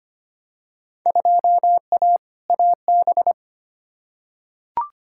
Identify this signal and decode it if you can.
Morse code, heard as a keyed tone, about 25 wpm, 700 Hz. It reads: 2A AB
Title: Field Day Exchanges - 25wpm